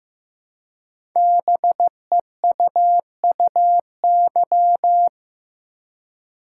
Morse code, heard as a keyed tone, about 15 wpm, 700 Hz. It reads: BEUUY
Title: Code Groups - 5 Letters - 15wpm